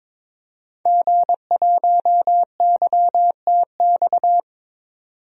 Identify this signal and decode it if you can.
Morse code, heard as a keyed tone, about 22 wpm, 700 Hz. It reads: G1YTX
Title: DXCC-100 Call Signs - 5 Characters - 22wpm